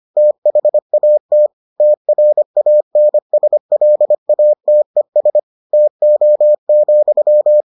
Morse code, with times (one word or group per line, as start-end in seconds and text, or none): 0.2-1.5 THAT
1.8-5.4 TRANSLATES
5.7-7.6 TO,